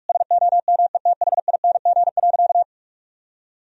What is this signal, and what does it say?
Morse code, heard as a keyed tone, about 46 wpm, 700 Hz. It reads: SOMETHING.
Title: Sentences from Top 1000 Words - Review Part 3 - 45wpm